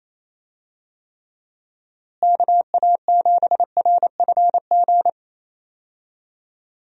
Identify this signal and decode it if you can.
Morse code, heard as a keyed tone, about 28 wpm, 700 Hz. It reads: KA7RFG
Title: USA Call Signs - 6 Characters - Repeated 2x - 28wpm